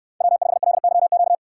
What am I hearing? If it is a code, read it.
Morse code, heard as a keyed tone, about 68 wpm, 700 Hz. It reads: W5LYQ